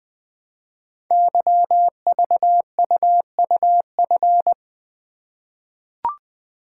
Morse code, heard as a keyed tone, about 20 wpm, 700 Hz. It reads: YVUUF
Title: Code Groups - 5 Letter:Numbers - 20wpm